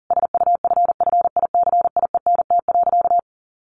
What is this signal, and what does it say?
Morse code, heard as a keyed tone, about 40 wpm, 700 Hz. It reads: SUFFICIENT.
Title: Sentences from Top 1500 Words 40wpm